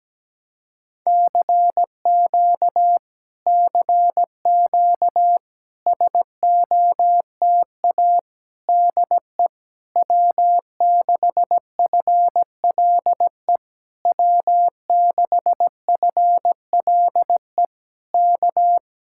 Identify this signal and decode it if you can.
Morse code, heard as a keyed tone, about 17 wpm, 700 Hz. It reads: CQ CQ SOTA DE W6FLE W6FLE K
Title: Speed-Racing Common QSO Phrases - 17wpm